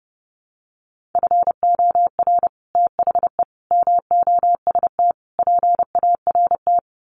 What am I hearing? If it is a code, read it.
Morse code, heard as a keyed tone, about 30 wpm, 700 Hz. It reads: FOR THE MOST PART
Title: Sets of 4 Words 30wpm